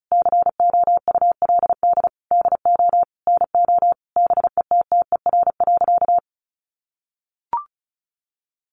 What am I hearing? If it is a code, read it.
Morse code, heard as a keyed tone, about 34 wpm, 700 Hz. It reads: COULD DO NO BETTER.
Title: Sentences from Top 1300 Words 35wpm